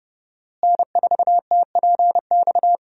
Morse code, heard as a keyed tone, about 30 wpm, 700 Hz. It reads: N4TPX
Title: USA Call Signs - 5 Characters - Repeated 3x - 30wpm